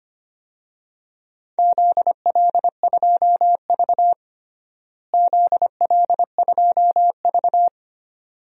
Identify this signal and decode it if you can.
Morse code, heard as a keyed tone, about 25 wpm, 700 Hz. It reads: ZL2V ZL2V
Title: DXCC-100 Call Signs - 4 Characters - Repeated 2x - 25wpm